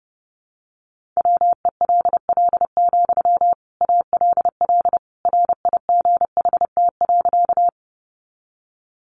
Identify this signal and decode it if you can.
Morse code, heard as a keyed tone, about 30 wpm, 700 Hz. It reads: WELL, ALL RIGHT.
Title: Sentences from Top 200 Words 30wpm